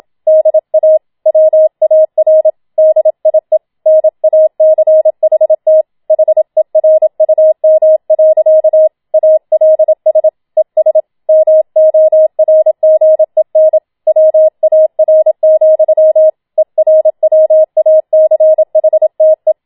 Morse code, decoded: DA WAR DIE NACHT HERUM. ALS ES MORGEN WAR, ERWACHTE

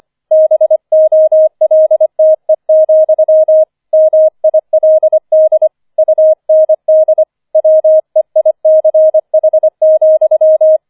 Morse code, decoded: BOLTE, MILD UND WEICH,

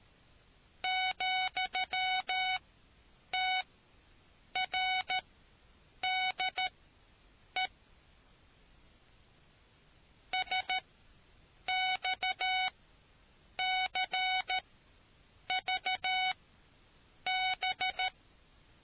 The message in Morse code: ,TRDE SXCVB

